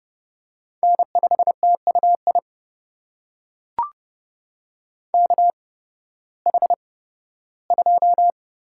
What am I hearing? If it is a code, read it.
Morse code, heard as a keyed tone, about 32 wpm, spaced as for 16 wpm, 700 Hz. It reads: N5TUI KH2
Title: USA Call Signs - 5 Characters - 8x Character Spacing - 30wpm